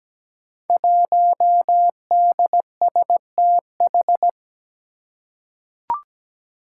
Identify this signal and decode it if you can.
Morse code, heard as a keyed tone, about 17 wpm, 700 Hz. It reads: 1DSTH
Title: Five Challenging Letter-Number Combinations 17wpm